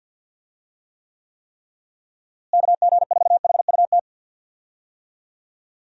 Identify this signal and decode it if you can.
Morse code, heard as a keyed tone, about 52 wpm, 700 Hz. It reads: KG4HUT
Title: USA Call Signs - 6 Characters - Repeated 3x - 50wpm